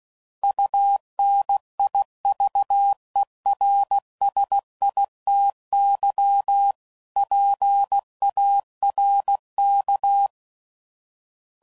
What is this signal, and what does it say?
Morse code, heard as a keyed tone, about 16 wpm, 800 Hz. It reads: UNIVERSITY PARK
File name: CW Code Groups @15WPM / fromSM's / Cities